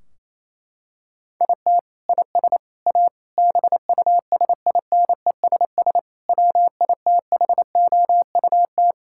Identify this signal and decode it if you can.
Morse code, heard as a keyed tone, about 28 wpm, 700 Hz. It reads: IT IS A BUSINESS WITHOUT
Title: Sentences from Top 1600 Words 28wpm